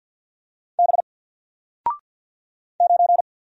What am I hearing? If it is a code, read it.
Morse code, heard as a keyed tone, about 48 wpm, 700 Hz. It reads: B 9
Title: Single Letter-Number Alt 50wpm